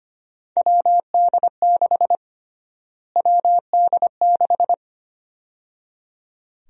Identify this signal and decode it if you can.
Morse code, heard as a keyed tone, about 25 wpm, 700 Hz. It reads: WD6 WD6
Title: USA Call Sign Prefixes - Repeated 2x - 25wpm